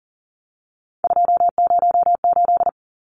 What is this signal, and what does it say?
Morse code, heard as a keyed tone, about 40 wpm, 700 Hz. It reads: 208
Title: Numbers - 3-Digit 40wpm